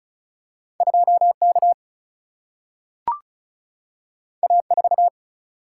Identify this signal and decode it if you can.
Morse code, heard as a keyed tone, about 36 wpm, 700 Hz. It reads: 2K A4
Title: Single-Letter Single-Number 35wpm